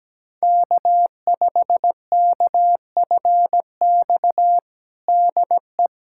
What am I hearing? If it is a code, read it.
Morse code, heard as a keyed tone, about 17 wpm, 700 Hz. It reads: K5KFX DE
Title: USA Callsign DE Callsign - 5 Characters - 17wpm